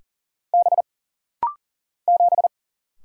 Morse code, heard as a keyed tone, about 40 wpm, 700 Hz. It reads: B 7
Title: Single Letter-Number Alt - Rapid-Fire 40wpm